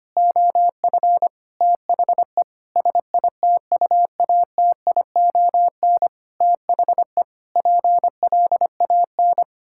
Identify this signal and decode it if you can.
Morse code, heard as a keyed tone, about 25 wpm, 700 Hz. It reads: OF THE SITUATION THE PLAN
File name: practice / 25wpm / Sentences with 9-Word Phrases - 25wpm